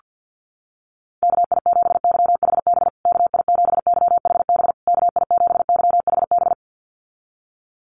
Morse code, heard as a keyed tone, about 48 wpm, 700 Hz. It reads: KI7YHB KI7YHB KI7YHB
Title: USA Call Signs Repeated 3x 50wpm